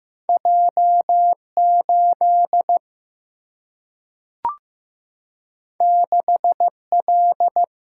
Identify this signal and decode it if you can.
Morse code, heard as a keyed tone, about 15 wpm, 700 Hz. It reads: J8 6L
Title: Single-Letter Single-Number 15wpm